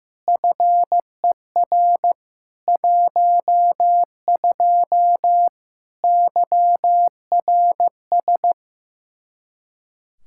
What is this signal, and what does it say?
Morse code, heard as a keyed tone, about 15 wpm, 700 Hz. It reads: FER 12 YRS
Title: Practice QSO - Exchange 3 - More Info - 15wpm